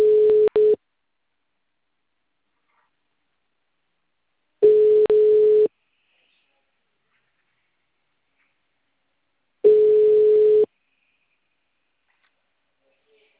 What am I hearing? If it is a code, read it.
Ringback tone (Europe, 425 Hz), with speech.